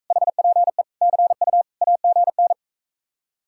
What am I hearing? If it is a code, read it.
Morse code, heard as a keyed tone, about 45 wpm, 700 Hz. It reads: HPE CU AGN